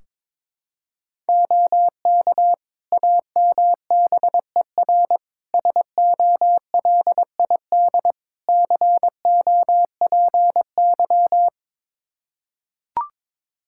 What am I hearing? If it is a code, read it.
Morse code, heard as a keyed tone, about 22 wpm, 700 Hz. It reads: OK AMBER SOLID COPY
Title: Practice QSO - Exchange 3 - More Info - 22wpm